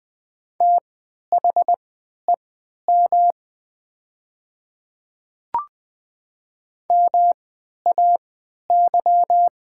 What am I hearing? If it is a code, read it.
Morse code, heard as a keyed tone, about 20 wpm, spaced as for 11 wpm, 700 Hz. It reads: THEM MAY
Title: 3x Character Spacing - Top 100 Words - 20wpm